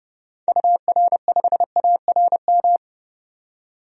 Morse code, heard as a keyed tone, about 30 wpm, 700 Hz. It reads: UR5ARM
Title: DXCC-100 Call Signs - 6 Characters - 30wpm